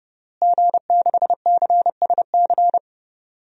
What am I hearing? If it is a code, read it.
Morse code, heard as a keyed tone, about 30 wpm, 700 Hz. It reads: G6CSC
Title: DXCC-100 Call Signs - 5 Characters - Repeated 3x - 30wpm